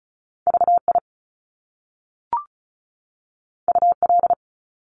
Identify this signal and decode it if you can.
Morse code, heard as a keyed tone, about 36 wpm, 700 Hz. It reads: VI UL
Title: Top Two Letter Combinations in Words 35wpm